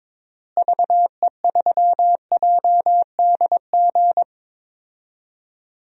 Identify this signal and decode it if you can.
Morse code, heard as a keyed tone, about 22 wpm, 700 Hz. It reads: VE3JDG
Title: DXCC-100 Call Signs - 6 Characters - Repeated 3x - 22wpm